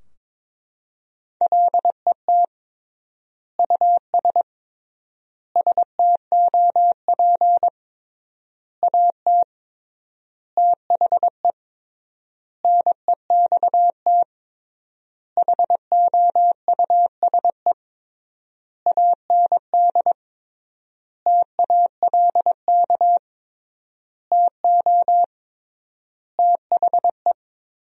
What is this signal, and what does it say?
Morse code, heard as a keyed tone, about 21 wpm, 700 Hz. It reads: LET US STOP AT THE NEXT HOUSE AND TALK TO THE
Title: 3x Word-Spacing - Sentences from Top 400 Words - 22wpm